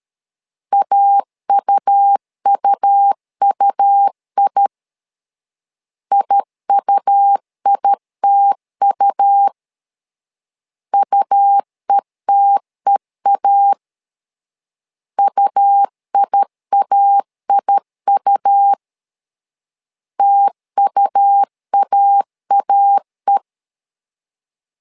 Morse code, heard as a keyed tone, about 13 wpm, 800 Hz. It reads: AUUUI IUITU UETEA UIAIU TUAAE
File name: LEZIONI / 2 UHDN / 2 U 12wpm 1338